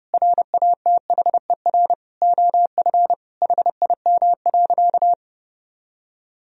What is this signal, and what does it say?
Morse code, heard as a keyed tone, about 30 wpm, 700 Hz. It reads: RATHER OF HIM.
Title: Sentences from Top 1000 Words - Review Part 4 - 30wpm